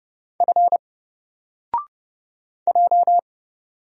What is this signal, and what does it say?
Morse code, heard as a keyed tone, about 30 wpm, 700 Hz. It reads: F J
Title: Single Character (Letter:Number:Punctuation:Prosign) 30wpm